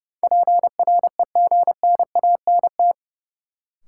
Morse code, heard as a keyed tone, about 31 wpm, 700 Hz. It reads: PREGNANT
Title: Sentences from Top 1000 Words - Review Part 6 - New Word Drill - 30wpm